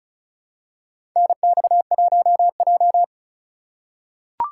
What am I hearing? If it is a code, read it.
Morse code, heard as a keyed tone, about 34 wpm, 700 Hz. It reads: NX1J
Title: USA Call Signs - 3x Character Spacing - 35wpm